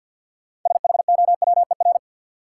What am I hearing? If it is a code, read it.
Morse code, heard as a keyed tone, about 52 wpm, 700 Hz. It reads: SHOWER
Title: Sentences from Top 1600 Words - New Word Drill - 50wpm